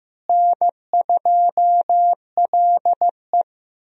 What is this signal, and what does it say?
Morse code, heard as a keyed tone, about 15 wpm, 700 Hz. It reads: N2LE
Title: USA Call Signs - 4 Characters - Repeated 3x - 15wpm